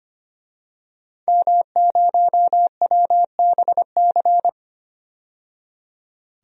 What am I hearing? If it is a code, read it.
Morse code, heard as a keyed tone, about 25 wpm, 700 Hz. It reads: M0WBC
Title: DXCC-100 Call Signs - 5 Characters - 25wpm